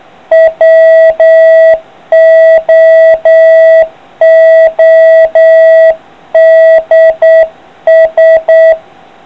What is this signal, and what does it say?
Morse code, heard as a keyed tone, about 8 wpm, 650 Hz. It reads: WOODS